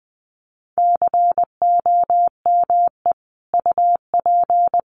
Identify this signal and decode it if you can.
Morse code, heard as a keyed tone, about 20 wpm, 700 Hz. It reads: COME UP
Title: Sets of 3 Words 20wpm